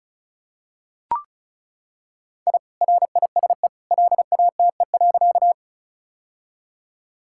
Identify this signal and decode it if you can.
Morse code, heard as a keyed tone, about 36 wpm, 700 Hz. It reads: I RISE LATE.
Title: Sentences from Top 1000 Words - Review Part 6 - 35wpm